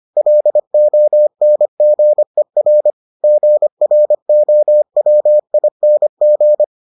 Morse code, heard as a keyed tone, about 25 wpm, 600 Hz. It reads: LONGER GROWING